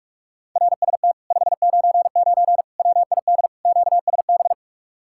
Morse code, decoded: RST 599 WID QSB

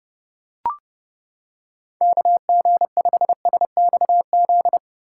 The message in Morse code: KG5SXZ

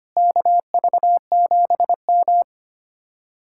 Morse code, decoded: KV7M

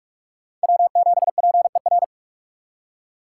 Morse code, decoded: W7PER

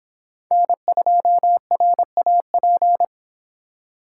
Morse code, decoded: N2RAP

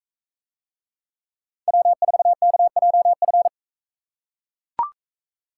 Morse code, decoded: W4KJF

